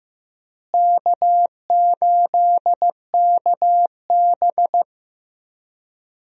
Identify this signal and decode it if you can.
Morse code, heard as a keyed tone, about 15 wpm, 700 Hz. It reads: K8KB
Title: USA Call Signs - 4 Characters - Repeated 2x - 15wpm